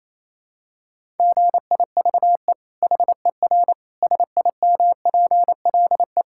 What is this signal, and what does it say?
Morse code, heard as a keyed tone, about 28 wpm, 700 Hz. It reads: GIVE HER SIMPLE
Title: Sentences from Top 1000 Words - Review Part 5 - 28wpm